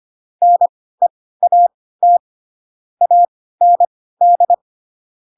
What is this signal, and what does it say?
Morse code, heard as a keyed tone, about 25 wpm, spaced as for 16 wpm, 700 Hz. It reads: NEAT AND
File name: Pinocchio / Pinocchio ch3 10 25 ik1hge